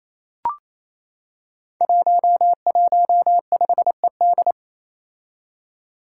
Morse code, heard as a keyed tone, about 28 wpm, 700 Hz. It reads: 115ED
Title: Five Challenging Letter-Number Combinations 28wpm